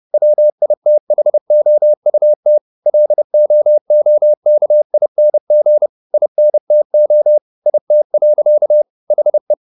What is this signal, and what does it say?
Morse code, heard as a keyed tone, about 30 wpm, 600 Hz. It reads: WITHOUT LOOKING INTO IT. HE
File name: the old man and the sea / the old man and the sea 1 30wpm